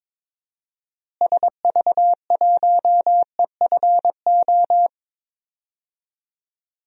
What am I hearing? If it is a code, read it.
Morse code, heard as a keyed tone, about 22 wpm, 700 Hz. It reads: SV1EFO
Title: DXCC-100 Call Signs - 6 Characters - Repeated 3x - 22wpm